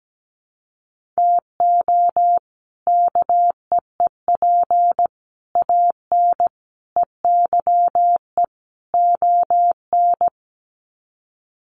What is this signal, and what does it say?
Morse code, heard as a keyed tone, about 17 wpm, 700 Hz. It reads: TO KEEP AN EYE ON
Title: Sets of 5 Words Encore 17wpm